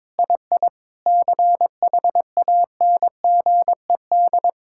Morse code, heard as a keyed tone, about 22 wpm, 700 Hz. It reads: II CHANGED
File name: Sentences with 7-Word Phrases - 22wpm